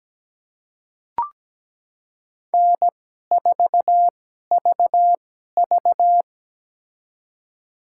Morse code, heard as a keyed tone, about 17 wpm, spaced as for 12 wpm, 700 Hz. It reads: N4VV
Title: USA Call Signs - 4 Characters - 2x Character Spacing - 17wpm